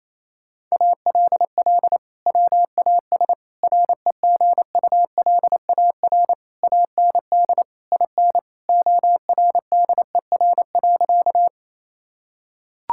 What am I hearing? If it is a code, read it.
Morse code, heard as a keyed tone, about 28 wpm, 700 Hz. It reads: ALL WAS REGULAR AND IN ORDER.
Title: Sentences from Top 1000 Words - Review Part 6 - 28wpm